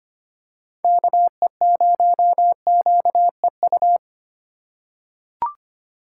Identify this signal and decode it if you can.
Morse code, heard as a keyed tone, about 25 wpm, 700 Hz. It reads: KE0QEU
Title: USA Call Signs - 6 Characters - 3x Character Spacing - 25wpm